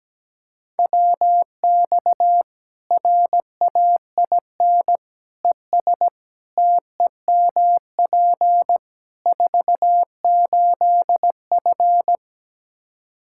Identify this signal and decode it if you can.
Morse code, heard as a keyed tone, about 17 wpm, 700 Hz. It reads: WX RAIN ES TEMP 48F
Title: Practice QSO - Exchange 2 - Info - 17wpm